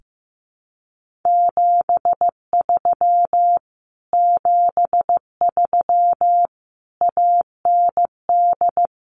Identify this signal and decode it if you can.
Morse code, heard as a keyed tone, about 15 wpm, 700 Hz. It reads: 73 73 AND
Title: Common QSO Phrases Encore 15wpm